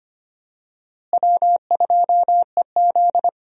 Morse code, heard as a keyed tone, about 25 wpm, 700 Hz. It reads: W2EZ
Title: USA Call Signs - 4 Characters - Repeated 3x - 25wpm